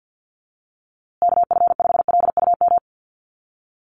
Morse code, heard as a keyed tone, about 48 wpm, 700 Hz. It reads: KF5LUM
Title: USA Call Signs Repeated 2x 50wpm